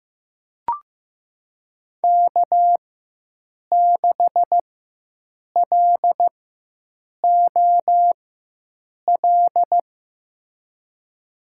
Morse code, heard as a keyed tone, about 15 wpm, spaced as for 7 wpm, 700 Hz. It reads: K6LOL
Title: USA Call Signs - 5 Characters - 4x Character Spacing - 15wpm